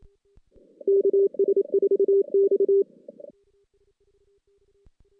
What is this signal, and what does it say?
Morse code, heard as a keyed tone, about 25 wpm, 410 Hz. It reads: KS4X
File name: 2016 JAN CW NAQP / 2016 Jan naqp k5ot - 0-5